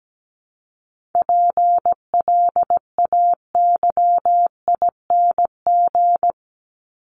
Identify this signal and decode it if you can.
Morse code, heard as a keyed tone, about 17 wpm, 700 Hz. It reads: PLAYING
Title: ING Suffix - Top 100 Words with Base Words 17wpm